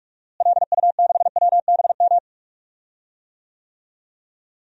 Morse code, heard as a keyed tone, about 41 wpm, 700 Hz. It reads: LU6WBM
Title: DXCC-100 Call Signs - 6 Characters - Repeated 2x - 45wpm